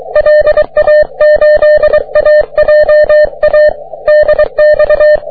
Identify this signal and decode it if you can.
Morse code, heard as a keyed tone, about 23 wpm, 600 Hz. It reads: LA8AJA DX
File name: wp / js6cqy 30m cw